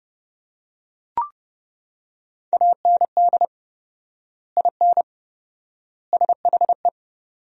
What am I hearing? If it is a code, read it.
Morse code, heard as a keyed tone, about 29 wpm, 700 Hz. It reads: AND IN SHE
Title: 4x Word-Spacing - Sentences from Top 200 Words - 30wpm